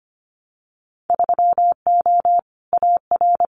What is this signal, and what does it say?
Morse code, heard as a keyed tone, about 25 wpm, 690 Hz. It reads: 3O AR
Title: Winter Field Day Exchanges Encore 25wpm